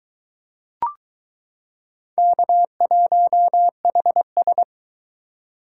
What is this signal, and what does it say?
Morse code, heard as a keyed tone, about 22 wpm, 700 Hz. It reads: K1HS